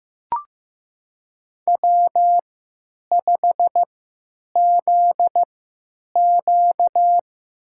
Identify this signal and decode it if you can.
Morse code, heard as a keyed tone, about 15 wpm, spaced as for 9 wpm, 700 Hz. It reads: W5ZQ
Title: USA Call Signs - 4 Characters - 3x Character Spacing - 15wpm